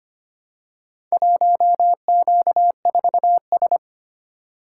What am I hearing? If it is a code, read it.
Morse code, heard as a keyed tone, about 25 wpm, 700 Hz. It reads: 1Q4S
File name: practice / 25wpm / Four Letter-Number Combinations - 25wpm